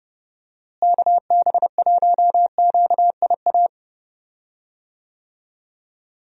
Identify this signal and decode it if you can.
Morse code, heard as a keyed tone, about 30 wpm, 700 Hz. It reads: KB1QIA
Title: USA Call Signs - 6 Characters - Repeated 3x - 30wpm